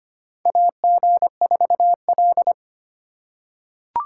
Morse code, heard as a keyed tone, about 25 wpm, 700 Hz. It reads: AG4L
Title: USA Call Signs - 3x Character Spacing - 25wpm